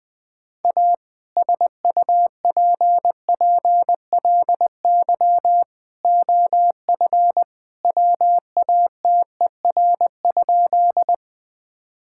Morse code, heard as a keyed tone, about 21 wpm, 700 Hz. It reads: A SUPPLY OF WATER?
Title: Sentences from Top 1000 Words - Review Part 5 - 20wpm